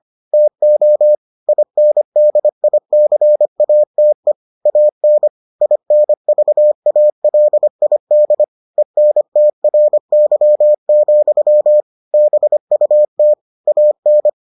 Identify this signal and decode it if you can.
Morse code, heard as a keyed tone, about 25 wpm, 600 Hz. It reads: TO INDICATE AN INVALID ENTRY, BUT AN